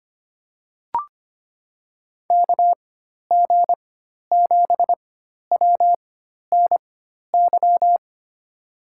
Morse code, heard as a keyed tone, about 25 wpm, spaced as for 12 wpm, 700 Hz. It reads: KG7WNY
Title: USA Call Signs - 6 Characters - 4x Character Spacing - 25wpm